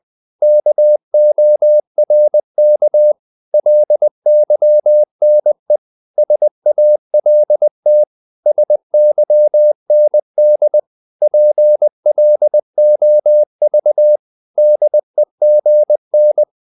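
Morse code, heard as a keyed tone, about 20 wpm, 600 Hz. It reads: KORK LYNE SALT SYND PLOV DEGN
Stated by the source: Morse code practice of short Danish words at 20 wpm, from CW med Gnister.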